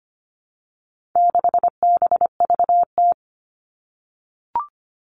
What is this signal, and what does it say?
Morse code, heard as a keyed tone, about 25 wpm, 700 Hz. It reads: 6BVT
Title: All Cut Numbers - 4-Digit 25wpm